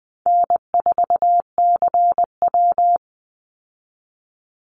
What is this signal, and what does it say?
Morse code, heard as a keyed tone, about 20 wpm, 700 Hz. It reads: N4CW
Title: Easy Call Signs Repeated 2x - 20wpm